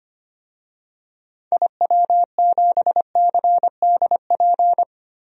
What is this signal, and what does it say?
Morse code, heard as a keyed tone, about 25 wpm, 700 Hz. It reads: IW7CDP
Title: DXCC-100 Call Signs - 6 Characters - 25wpm